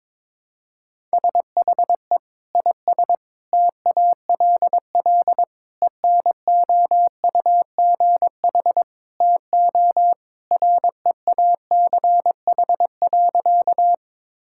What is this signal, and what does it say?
Morse code, heard as a keyed tone, about 21 wpm, 700 Hz. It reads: SHE IS TALL ENOUGH TO REACH.
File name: practice / 22wpm / Sentences with 2-Word Phrases - 22wpm